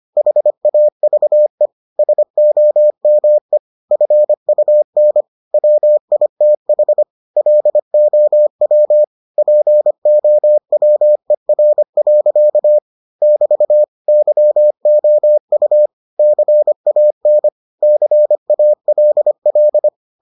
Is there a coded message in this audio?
Yes: Morse code from 0.2 to 19.9 s, heard as a keyed tone, about 25 wpm, 600 Hz. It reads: HAVE SOME FUN WITH LOW POWER. <BT> YOU CAN CALL